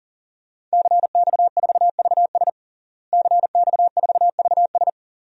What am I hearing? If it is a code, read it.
Morse code, heard as a keyed tone, about 40 wpm, 700 Hz. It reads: CX4VS CX4VS
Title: DXCC-100 Call Signs - 5 Characters - Repeated 3x - 40wpm